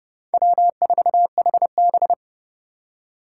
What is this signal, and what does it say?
Morse code, heard as a keyed tone, about 29 wpm, 700 Hz. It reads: W4HB